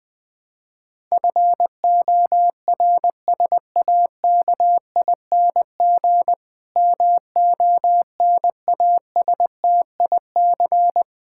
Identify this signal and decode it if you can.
Morse code, heard as a keyed tone, about 20 wpm, 700 Hz. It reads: FORSAKING MONASTIC
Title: Pangram Challenge v2 - 20wpm